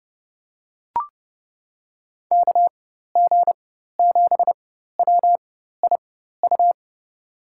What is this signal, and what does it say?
Morse code, heard as a keyed tone, about 30 wpm, spaced as for 14 wpm, 700 Hz. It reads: KG7WIU
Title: USA Call Signs - 6 Characters - 4x Character Spacing - 30wpm